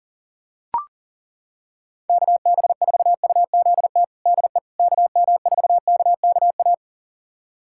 Morse code, heard as a keyed tone, about 40 wpm, 700 Hz. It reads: KB4UZT DE KM4KKA
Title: USA Callsign DE Callsign - 6 Characters - 40wpm